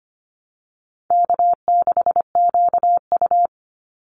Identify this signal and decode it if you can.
Morse code, heard as a keyed tone, about 25 wpm, 700 Hz. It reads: K6QU
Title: Easy Call Signs - 25wpm